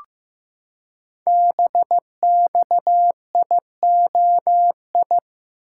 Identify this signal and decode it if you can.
Morse code, heard as a keyed tone, about 15 wpm, 700 Hz. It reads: BXIOI
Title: Code Groups - 5 Letters - 15wpm